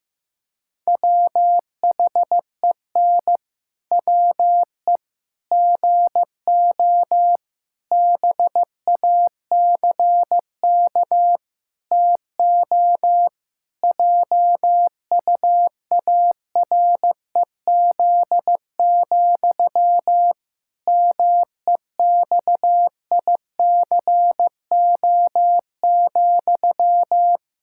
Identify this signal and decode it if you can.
Morse code, heard as a keyed tone, about 15 wpm, 700 Hz. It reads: WHEN WE GO BACK TO JUAREZ, MEXICO,
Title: Pangram Challenge v2 - 15wpm